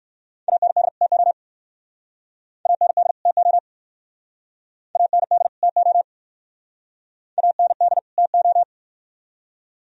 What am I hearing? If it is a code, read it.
Morse code, heard as a keyed tone, about 60 wpm, 700 Hz. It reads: AND TO AND TO AND TO AND TO